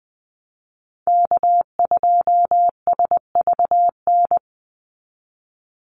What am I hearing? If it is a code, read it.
Morse code, heard as a keyed tone, about 20 wpm, 700 Hz. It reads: K2SVN
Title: USA Call Signs 20wpm